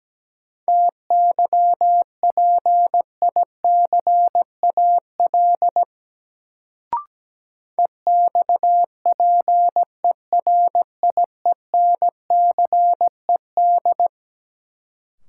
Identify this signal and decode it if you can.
Morse code, heard as a keyed tone, about 17 wpm, 700 Hz. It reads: TYPICAL EXPERIENCED
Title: Sentences from Top 1100 Words - New Word Drill - 17wpm